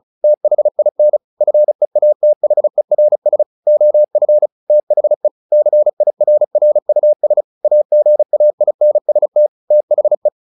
Morse code, heard as a keyed tone, about 34 wpm, 600 Hz. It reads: THIN FEATHERS OF THE CIRRUS AGAINST THE
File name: the old man and the sea / the old man and the sea 1 35wpm